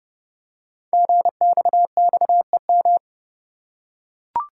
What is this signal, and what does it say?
Morse code, heard as a keyed tone, about 29 wpm, 690 Hz. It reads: GXXEM
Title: Code Groups - 5 Letters - 30wpm